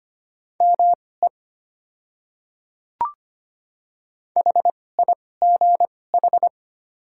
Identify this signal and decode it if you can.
Morse code, heard as a keyed tone, about 25 wpm, spaced as for 18 wpm, 700 Hz. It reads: ME HIGH
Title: 2x Character Spacing - Top 300 Words - 25wpm